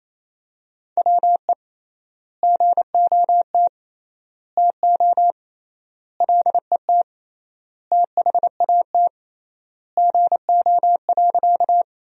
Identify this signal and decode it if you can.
Morse code, heard as a keyed tone, about 28 wpm, 700 Hz. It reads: WE GOT TO LET THAT GO.
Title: 3x Word-Spacing - Sentences from Top 300 Words - 28wpm